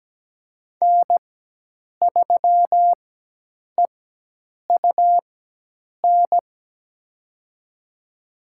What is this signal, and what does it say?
Morse code, heard as a keyed tone, about 17 wpm, spaced as for 8 wpm, 700 Hz. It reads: N3EUN
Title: USA Call Signs - 5 Characters - 4x Character Spacing - 17wpm